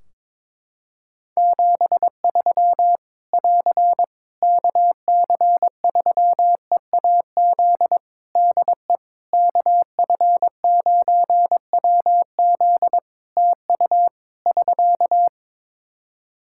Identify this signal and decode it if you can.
Morse code, heard as a keyed tone, about 22 wpm, 700 Hz. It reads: 73 <AR> KC3EAZ DE KF9WZ TU <SK>
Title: Practice QSO - Exchange 4 - Ending - 22wpm